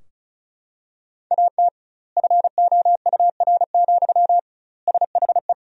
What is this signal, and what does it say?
Morse code, heard as a keyed tone, about 34 wpm, 700 Hz. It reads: AT FOUR, SHE
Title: 2x Word-Spacing - Sentences from Top 300 Words - 35wpm